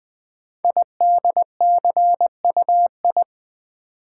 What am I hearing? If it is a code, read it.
Morse code, heard as a keyed tone, about 20 wpm, 700 Hz. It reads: IDCUI
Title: Code Groups - 5 Letter:Numbers - 20wpm